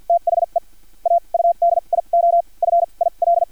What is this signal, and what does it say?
Morse code, heard as a keyed tone, about 50 wpm, spaced as for 38 wpm, 690 Hz. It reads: THE AUDIO FIL
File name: archived / CRSnet